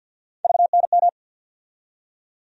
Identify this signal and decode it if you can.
Morse code, heard as a keyed tone, about 47 wpm, 700 Hz. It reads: VNM
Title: USA Call Sign Postfixes - 3 Letters 50wpm